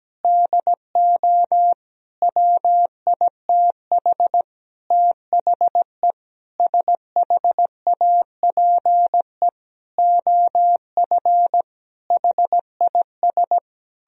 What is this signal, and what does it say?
Morse code, heard as a keyed tone, about 17 wpm, 700 Hz. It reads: DO WITH THE SHAPE OF HIS
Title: Sentences from Top 1000 Words - Review Part 3 - 17wpm